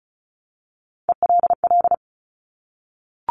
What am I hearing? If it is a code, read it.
Morse code, heard as a keyed tone, about 36 wpm, 700 Hz. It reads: ELL
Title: Top Three Letter Combinations in Words 35wpm